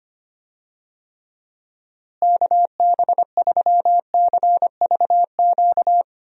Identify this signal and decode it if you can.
Morse code, heard as a keyed tone, about 25 wpm, 700 Hz. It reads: KB3CVQ
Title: USA Call Signs - 6 Characters - Repeated 3x - 25wpm